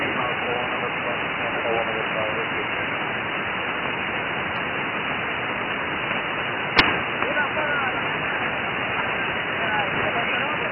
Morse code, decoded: none

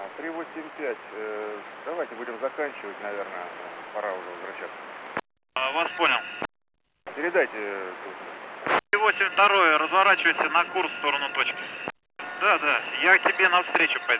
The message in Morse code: none

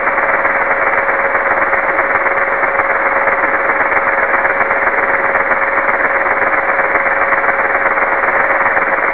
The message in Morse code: none

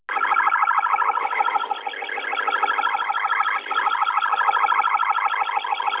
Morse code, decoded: none

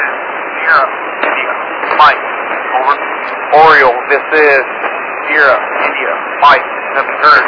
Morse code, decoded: none

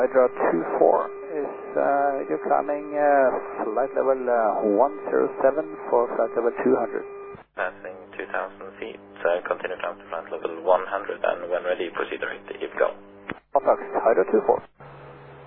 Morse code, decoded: none

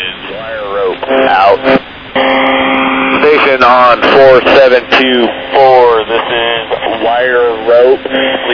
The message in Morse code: none

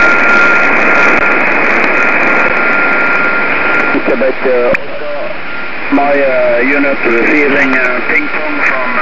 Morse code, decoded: none

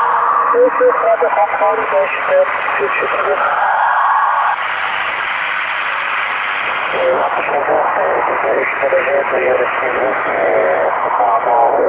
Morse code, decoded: none

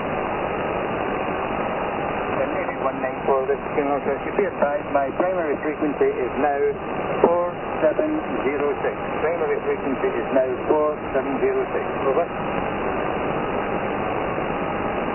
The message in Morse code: none

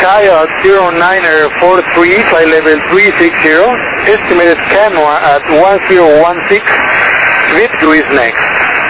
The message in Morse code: none

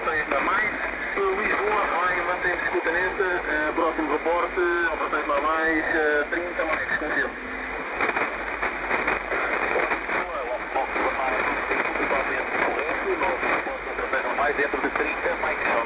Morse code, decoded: none

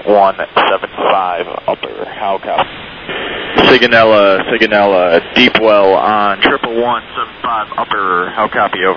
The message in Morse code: none

